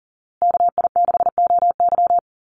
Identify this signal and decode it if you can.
Morse code, heard as a keyed tone, about 37 wpm, 700 Hz. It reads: KI6OY
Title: Call Signs Repeated 2x - 40wpm